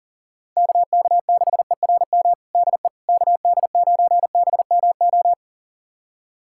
Morse code, heard as a keyed tone, about 37 wpm, 700 Hz. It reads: KK6ERM DE KD9BMO
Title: USA Callsign DE Callsign - 6 Characters - 40wpm